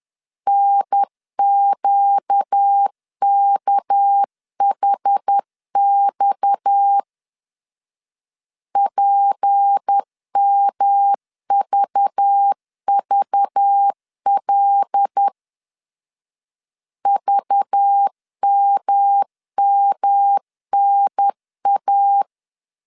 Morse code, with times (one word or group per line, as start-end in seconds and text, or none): 0.5-7.0 NQKHX
8.8-15.3 PMVVL
17.1-22.2 VMMNA